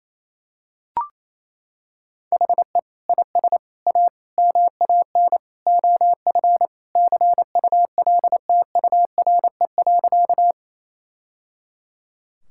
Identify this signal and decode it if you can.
Morse code, heard as a keyed tone, about 28 wpm, 700 Hz. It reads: HE IS A MAN OF CULTURE.
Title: Sentences from Top 1000 Words - Review Part 6 - 28wpm